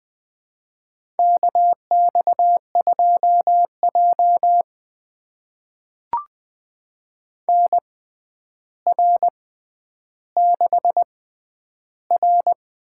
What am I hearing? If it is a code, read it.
Morse code, heard as a keyed tone, about 20 wpm, spaced as for 10 wpm, 700 Hz. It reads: KX2J NR6R
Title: USA Call Signs - 4 Characters - 6x Character Spacing - 20wpm